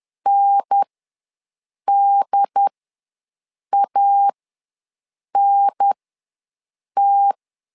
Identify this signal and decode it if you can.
Morse code, heard as a keyed tone, about 10 wpm, spaced as for 6 wpm, 800 Hz. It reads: NDANT
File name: LEZIONI / 2 UHDN / 2 UHDN ws 10wpm 2044